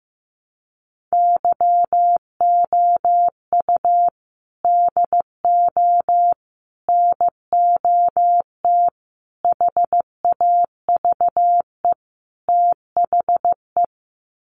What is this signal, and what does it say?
Morse code, heard as a keyed tone, about 15 wpm, 700 Hz. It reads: YOU DO NOT HAVE THE
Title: Sets of 5 Words 15wpm